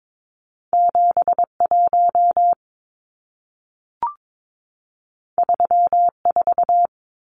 Morse code, heard as a keyed tone, about 22 wpm, 700 Hz. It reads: 71 34
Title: Numbers - 2-Digit 22wpm